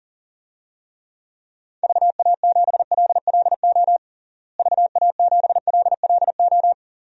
Morse code, heard as a keyed tone, about 40 wpm, 700 Hz. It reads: VA7LLO VA7LLO
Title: DXCC-100 Call Signs - 6 Characters - Repeated 2x - 40wpm